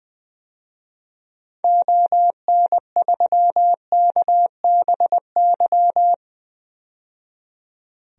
Morse code, heard as a keyed tone, about 21 wpm, 700 Hz. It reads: ON3KBY
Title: DXCC-100 Call Signs - 6 Characters - 20wpm